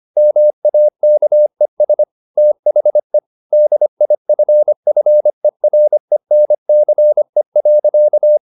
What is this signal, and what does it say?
Morse code, heard as a keyed tone, about 25 wpm, 600 Hz. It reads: MAKES THE DIFFERENCE.
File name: the old man and the sea part 1 25wpm